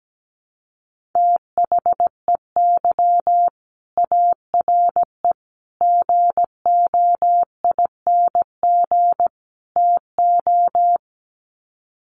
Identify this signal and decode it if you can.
Morse code, heard as a keyed tone, about 17 wpm, 700 Hz. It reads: THEY ARE GOING TO
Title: Sets of 4 Words 17wpm